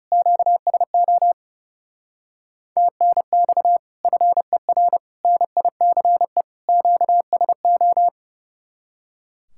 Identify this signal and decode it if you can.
Morse code, heard as a keyed tone, about 39 wpm, 700 Hz. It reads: QSO TNX FER NICE QSO